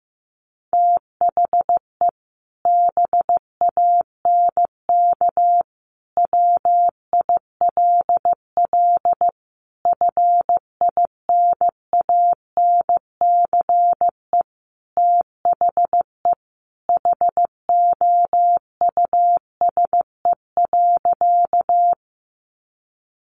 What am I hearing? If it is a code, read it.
Morse code, heard as a keyed tone, about 15 wpm, 700 Hz. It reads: THE BANK WILL FINANCE THE HOUSE.
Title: Sentences from Top 1500 Words 15wpm